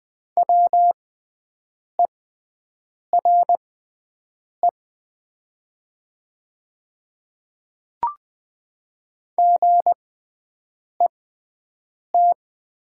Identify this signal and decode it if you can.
Morse code, heard as a keyed tone, about 20 wpm, spaced as for 7 wpm, 700 Hz. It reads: WERE GET
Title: 6x Character Spacing - Top 200 Words - 20wpm